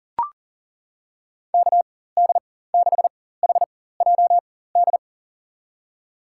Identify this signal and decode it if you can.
Morse code, heard as a keyed tone, about 38 wpm, spaced as for 18 wpm, 700 Hz. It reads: KD6HJD
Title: USA Call Signs - 6 Characters - 4x Character Spacing - 40wpm